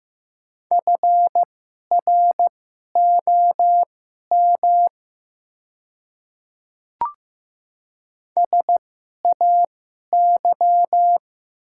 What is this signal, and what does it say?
Morse code, heard as a keyed tone, about 15 wpm, spaced as for 11 wpm, 700 Hz. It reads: FROM SAY
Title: 2x Character Spacing - Top 300 Words - 15wpm